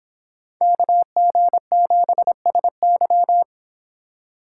Morse code, heard as a keyed tone, about 26 wpm, 700 Hz. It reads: KG7SY